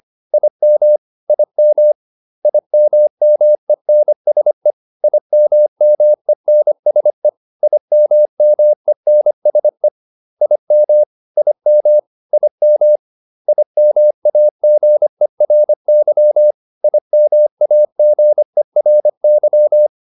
Morse code, heard as a keyed tone, about 25 wpm, 600 Hz. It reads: IM IM IMMENSE IMMENSE IMMENSE IM IM IM IMAGERY IMAGERY